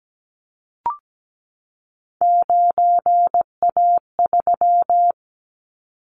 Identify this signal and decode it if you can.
Morse code, heard as a keyed tone, about 17 wpm, 700 Hz. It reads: 9A3
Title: Callsign Prefixes for Top 50 DXCC - 17wpm